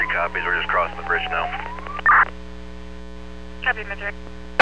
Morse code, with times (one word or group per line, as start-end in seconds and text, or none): none